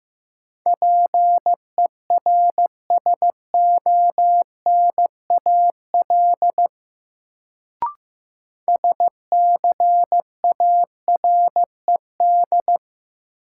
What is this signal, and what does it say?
Morse code, heard as a keyed tone, about 15 wpm, 700 Hz. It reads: PERSONAL SCARED
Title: Sentences from Top 700 Words - New Word Drill - 15wpm